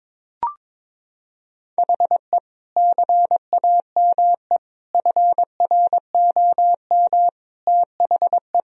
Morse code, heard as a keyed tone, about 21 wpm, 700 Hz. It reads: HE CAME FROM THE
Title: Sentences from Top 1000 Words - Review Part 1 - 22wpm